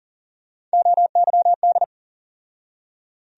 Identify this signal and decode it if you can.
Morse code, heard as a keyed tone, about 40 wpm, 700 Hz. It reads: OYD
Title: USA Call Sign Postfixes - 3 Letters 40wpm